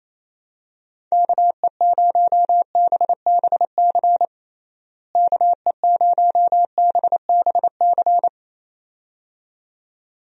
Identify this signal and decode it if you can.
Morse code, heard as a keyed tone, about 28 wpm, 700 Hz. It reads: KE0BBC KE0BBC
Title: USA Call Signs - 6 Characters - Repeated 2x - 28wpm